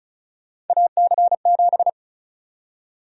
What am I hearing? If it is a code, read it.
Morse code, heard as a keyed tone, about 36 wpm, 700 Hz. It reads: AC7
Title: USA Call Sign Prefixes - 35wpm